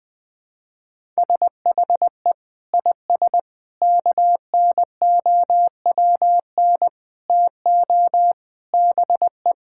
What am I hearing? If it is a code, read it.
Morse code, heard as a keyed tone, about 20 wpm, 700 Hz. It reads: SHE IS KNOWN TO BE
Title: Sentences from Top 1000 Words - Review Part 5 - 20wpm